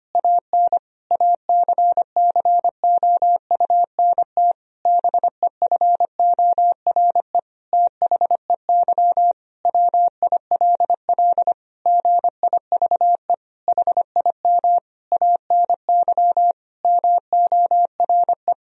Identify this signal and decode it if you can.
Morse code, heard as a keyed tone, about 25 wpm, 700 Hz. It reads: AN ACCOUNT BEFORE THEY WILL GIVE HIM ANY MORE
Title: Sentences from Top 1000 Words - Review Part 4 - 25wpm